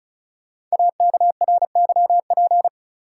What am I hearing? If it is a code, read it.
Morse code, heard as a keyed tone, about 36 wpm, 700 Hz. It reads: AKRYP
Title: Code Groups - 5 Letters - 35wpm